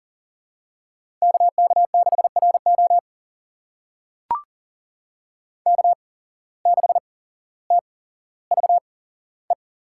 Morse code, heard as a keyed tone, about 40 wpm, spaced as for 17 wpm, 700 Hz. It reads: KK6RO K6TVE
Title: USA Call Signs - 5 Characters - 8x Character Spacing - 40wpm